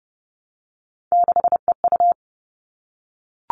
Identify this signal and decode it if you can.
Morse code, heard as a keyed tone, about 30 wpm, 700 Hz. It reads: 6EU
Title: All Cut Numbers - 3-Digit 30wpm